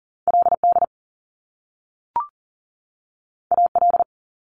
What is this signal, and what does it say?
Morse code, heard as a keyed tone, about 40 wpm, 700 Hz. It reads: LD AL
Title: Top Two Letter Combinations in Words 40wpm